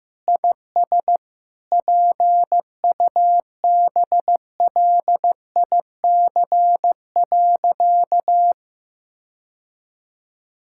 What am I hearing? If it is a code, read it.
Morse code, heard as a keyed tone, about 15 wpm, 700 Hz. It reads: IS PUBLIC.
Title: Sentences from Top 1000 Words - Review Part 5 - 15wpm